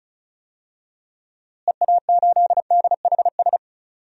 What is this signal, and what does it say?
Morse code, heard as a keyed tone, about 34 wpm, 700 Hz. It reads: EA8DHS
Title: DXCC-100 Call Signs - 6 Characters - Repeated 2x - 35wpm